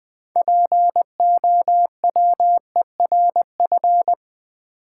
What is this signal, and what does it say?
Morse code, heard as a keyed tone, about 20 wpm, 700 Hz. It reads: POWERF
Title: Build A Word - 2 Letters At A Time - 8-Letter Words - 20wpm